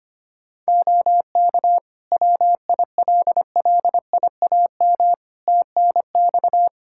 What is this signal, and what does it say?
Morse code, heard as a keyed tone, about 25 wpm, 700 Hz. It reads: OK WILLIAM TNX
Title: Practice QSO - Exchange 4 - Ending - 25wpm